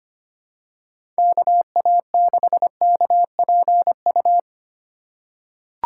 Morse code, heard as a keyed tone, about 25 wpm, 700 Hz. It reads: KA6KPU
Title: USA Call Signs - 3x Character Spacing - 25wpm